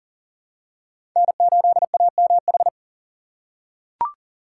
Morse code, heard as a keyed tone, about 40 wpm, 700 Hz. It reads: N8AMH
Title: USA Call Signs - 5 Characters - 4x Character Spacing - 40wpm